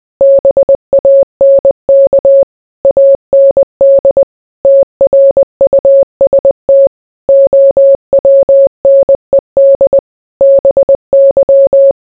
Morse code, heard as a keyed tone, about 20 wpm, 550 Hz. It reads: BANK AND TRUST OWNED BY